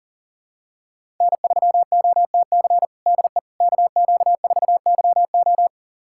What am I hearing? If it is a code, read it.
Morse code, heard as a keyed tone, about 40 wpm, 700 Hz. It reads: N3OTC DE KQ4YO
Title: USA Callsign DE Callsign - 5 Characters - 40wpm